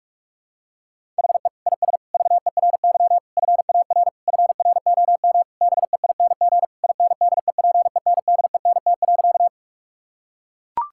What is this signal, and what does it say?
Morse code, heard as a keyed tone, about 41 wpm, 700 Hz. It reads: HE IS VERY FAR FROM BEING INDEPENDENT.
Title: Sentences from Top 1000 Words - Review Part 6 - 45wpm